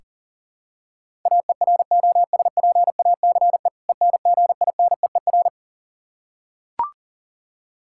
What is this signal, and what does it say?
Morse code, heard as a keyed tone, about 40 wpm, 700 Hz. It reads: AEROSPACE ENGINEER
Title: Practice QSO - Exchange 3 - More Info - 40wpm